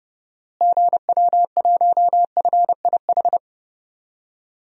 Morse code, heard as a keyed tone, about 30 wpm, 700 Hz. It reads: GW1FIH
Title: DXCC-100 Call Signs - 6 Characters - Repeated 3x - 30wpm